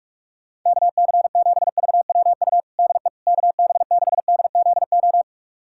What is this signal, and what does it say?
Morse code, heard as a keyed tone, about 47 wpm, 700 Hz. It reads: KC7VWU DE KB6DZO
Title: USA Callsign DE Callsign 45wpm